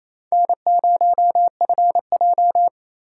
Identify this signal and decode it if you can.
Morse code, heard as a keyed tone, about 37 wpm, 700 Hz. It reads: N0FJ